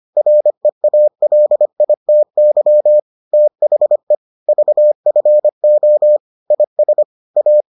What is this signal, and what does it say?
Morse code, heard as a keyed tone, about 25 wpm, 600 Hz. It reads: REALITY THE VFO IS A